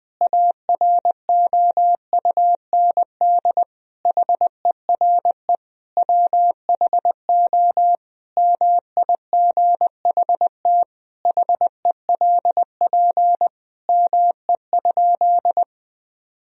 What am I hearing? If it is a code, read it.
Morse code, heard as a keyed tone, about 20 wpm, 700 Hz. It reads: AROUND HERE WHO MIGHT HELP ME?
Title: Sentences from Top 1000 Words - Review Part 3 - 20wpm